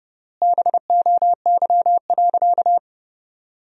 Morse code, heard as a keyed tone, about 30 wpm, 700 Hz. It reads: BOY.
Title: 3x Word-Spacing - Sentences from Top 200 Words - 30wpm